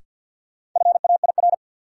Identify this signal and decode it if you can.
Morse code, heard as a keyed tone, about 53 wpm, 700 Hz. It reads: FAIR
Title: Sentences from Top 1300 Words - New Word Drill Alt - 50wpm